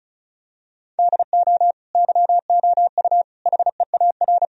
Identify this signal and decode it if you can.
Morse code, heard as a keyed tone, about 34 wpm, 700 Hz. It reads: DO YOU HEAR
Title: Sentences from Top 1000 Words - Review Part 1 - 35wpm